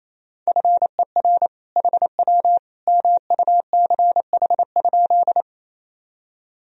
Morse code, heard as a keyed tone, about 28 wpm, 700 Hz. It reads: FER HW MUCH?
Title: Sentences from Top 200 Words and Abbreviations 28wpm